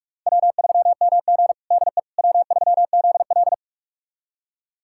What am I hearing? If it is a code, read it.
Morse code, heard as a keyed tone, about 42 wpm, 700 Hz. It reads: W3MG DE W3ZL
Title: USA Callsign DE Callsign - 4 Characters - 45wpm